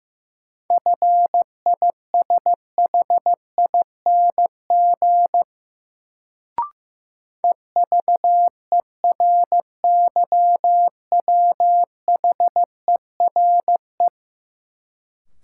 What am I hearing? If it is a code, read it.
Morse code, heard as a keyed tone, about 15 wpm, 700 Hz. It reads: FISHING EVERYWHERE
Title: Sentences from Top 1600 Words - New Word Drill Alt - 15wpm